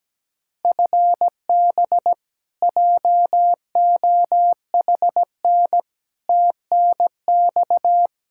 Morse code, heard as a keyed tone, about 17 wpm, 700 Hz. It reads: FB JOHN TNX
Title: Practice QSO - Exchange 3 - More Info - 17wpm